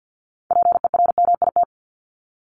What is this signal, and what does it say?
Morse code, heard as a keyed tone, about 50 wpm, 700 Hz. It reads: PERMIT
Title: Top 1751-2000 Words 50wpm